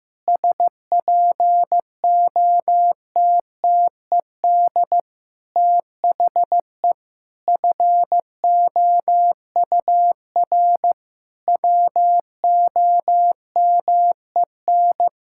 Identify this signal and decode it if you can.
Morse code, heard as a keyed tone, about 15 wpm, 700 Hz. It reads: SPOTTED THE FOUR WOMEN
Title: Pangram Challenge v2 - 15wpm